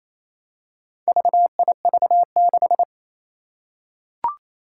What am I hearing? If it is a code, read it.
Morse code, heard as a keyed tone, about 27 wpm, 700 Hz. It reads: VIV6
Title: Four Challenging Letter-Number Combinations 28wpm